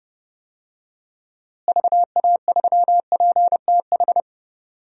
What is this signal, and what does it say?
Morse code, heard as a keyed tone, about 31 wpm, 700 Hz. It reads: VA3PTH
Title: DXCC-100 Call Signs - 6 Characters - Repeated 2x - 30wpm